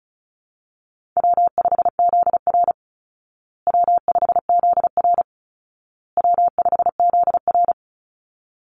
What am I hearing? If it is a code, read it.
Morse code, heard as a keyed tone, about 34 wpm, 700 Hz. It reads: W5ZR W5ZR W5ZR
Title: Easy Call Signs Repeated 3x - 35wpm